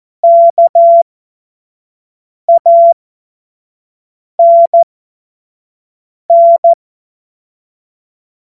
Morse code, heard as a keyed tone, about 14 wpm, spaced as for 5 wpm, 680 Hz. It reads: KANN
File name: Morsekurs / Witze / Witz 12 März 2024 25 F70